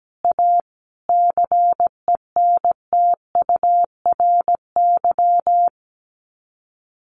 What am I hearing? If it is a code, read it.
Morse code, heard as a keyed tone, about 17 wpm, 700 Hz. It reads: A CENTURY
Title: Sets of 5 Words Encore2 17wpm